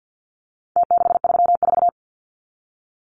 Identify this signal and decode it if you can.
Morse code, heard as a keyed tone, about 50 wpm, 700 Hz. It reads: T634
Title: Numbers - Contest Style - 4-Digit 50wpm